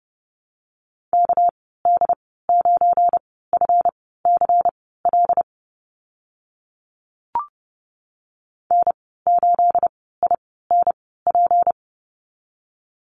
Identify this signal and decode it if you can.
Morse code, heard as a keyed tone, about 30 wpm, spaced as for 17 wpm, 700 Hz. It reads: KD9FCL N8INP
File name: Lesson 37g - Letter x - Call Signs 30wpm